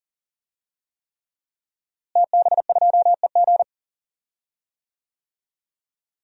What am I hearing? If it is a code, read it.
Morse code, heard as a keyed tone, about 40 wpm, 700 Hz. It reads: TB2EG
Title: DXCC-100 Call Signs - 5 Characters - Repeated 3x - 40wpm